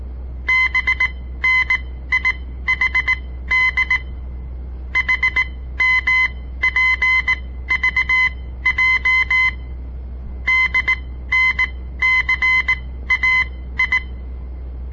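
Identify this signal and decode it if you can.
Morse code, heard as a keyed tone, about 18 wpm, spaced as for 13 wpm, 2050 Hz. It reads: BNIHD HMPVJ DNCAI
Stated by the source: On-air Morse code practice broadcast by a southern Montana amateur radio association; all code, no read back.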